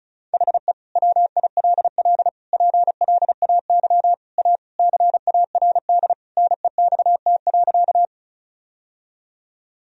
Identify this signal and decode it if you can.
Morse code, heard as a keyed tone, about 36 wpm, 700 Hz. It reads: HE WILL PLAY A CARD NEXT.
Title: Sentences from Top 1000 Words - Review Part 4 - 35wpm